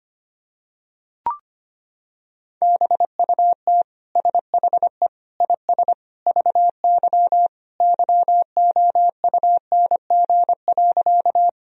Morse code, heard as a keyed tone, about 25 wpm, 700 Hz. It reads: BUT SHE IS VY YOUNG.
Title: Sentences from Top 400 Words and Abbreviations 25wpm